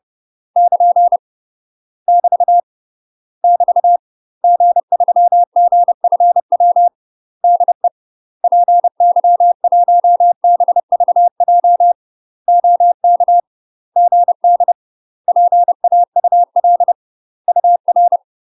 Morse code, heard as a keyed tone, about 30 wpm, 700 Hz. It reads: <KN> <BT> <BT> G3GFW DE PY1BVJ OK GD PAUL UR
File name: UK MASTER QSOs TRAINING/30WPM